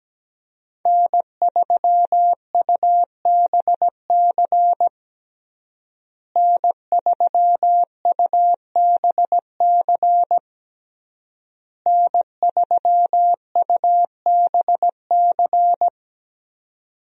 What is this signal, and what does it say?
Morse code, heard as a keyed tone, about 17 wpm, 700 Hz. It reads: N3UBC N3UBC N3UBC
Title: USA Call Signs - 5 Characters - Repeated 3x - 17wpm